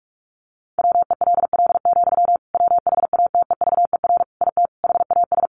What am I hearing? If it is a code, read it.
Morse code, heard as a keyed tone, about 44 wpm, 700 Hz. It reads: WELL, WHATEVER IT HAS
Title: Sentences from Top 1100 Words 45wpm